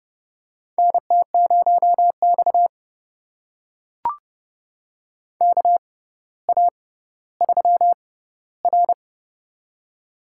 Morse code, heard as a keyed tone, about 30 wpm, spaced as for 15 wpm, 700 Hz. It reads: NT0X KA3R
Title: USA Call Signs - 4 Characters - 6x Character Spacing - 30wpm